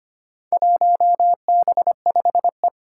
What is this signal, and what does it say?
Morse code, heard as a keyed tone, about 25 wpm, 700 Hz. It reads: 1B5E
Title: Four Challenging Letter-Number Combinations 25wpm